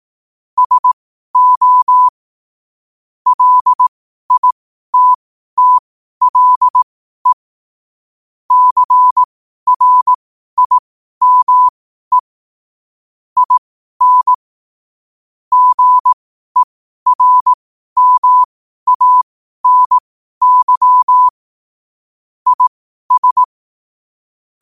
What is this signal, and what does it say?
Morse code, heard as a keyed tone, about 18 wpm, spaced as for 12 wpm, 1000 Hz. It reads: SO LITTLE CRIME IN GERMANY IS
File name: qotdfiles / 12wpm